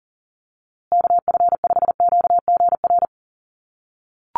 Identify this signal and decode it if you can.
Morse code, heard as a keyed tone, about 40 wpm, 700 Hz. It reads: KF5QGR
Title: USA Call Signs Repeated 3x 40wpm